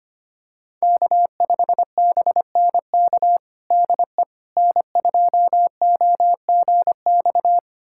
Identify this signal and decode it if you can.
Morse code, heard as a keyed tone, about 25 wpm, 700 Hz. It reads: K5BNK DE N2OGX
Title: USA Callsign DE Callsign - 5 Characters - 25wpm